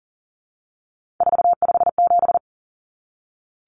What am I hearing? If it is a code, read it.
Morse code, heard as a keyed tone, about 40 wpm, 700 Hz. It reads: 457
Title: Common Cut Numbers - 3-Digit 40wpm